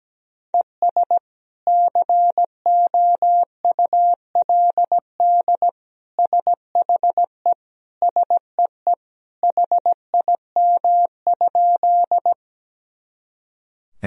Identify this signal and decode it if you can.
Morse code, heard as a keyed tone, about 17 wpm, 700 Hz. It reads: ES COULD SHE SEE HIM?
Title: Sentences from Top 100 Words and Abbreviations 17wpm